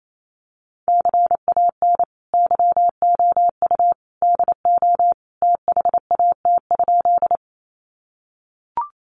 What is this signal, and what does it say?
Morse code, heard as a keyed tone, about 28 wpm, 700 Hz. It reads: CAN YOU DO THAT?
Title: Sentences from Top 100 Words 28wpm